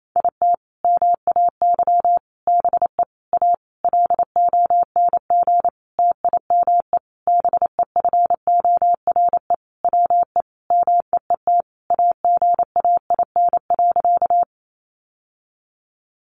Morse code, heard as a keyed tone, about 28 wpm, 700 Hz. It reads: IT MAY BE A LONG TIME BEFORE WE MEET AGAIN.
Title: Sentences from Top 1100 Words 28wpm